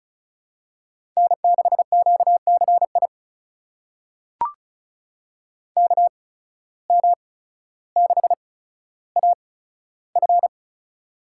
Morse code, heard as a keyed tone, about 34 wpm, spaced as for 15 wpm, 700 Hz. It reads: N6QCI KM6AF
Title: USA Call Signs - 5 Characters - 8x Character Spacing - 35wpm